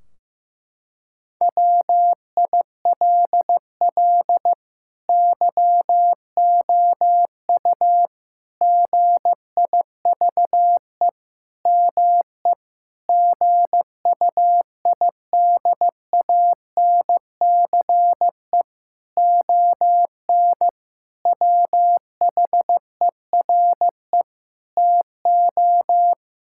Morse code, heard as a keyed tone, about 15 wpm, 700 Hz. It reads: WILL YOU GIVE ME GUIDANCE ON WHERE TO
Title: Sentences from Top 1600 Words 15wpm